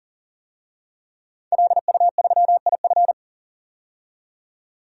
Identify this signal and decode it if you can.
Morse code, heard as a keyed tone, about 40 wpm, 700 Hz. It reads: LU3IF
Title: DXCC-100 Call Signs - 5 Characters - 40wpm